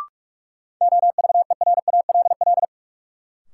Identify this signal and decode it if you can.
Morse code, heard as a keyed tone, about 46 wpm, 700 Hz. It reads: OVERALL
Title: Sentences from Top 1000 Words - Review Part 7 - New Word Drill - 45wpm